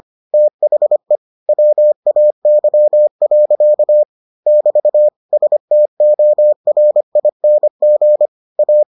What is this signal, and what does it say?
Morse code, heard as a keyed tone, about 25 wpm, 600 Hz. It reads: THE WAY. <BT> STORING A